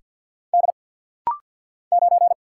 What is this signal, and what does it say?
Morse code, heard as a keyed tone, about 48 wpm, 690 Hz. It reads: D 9
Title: Single Letter-Number - Rapid-Fire 50wpm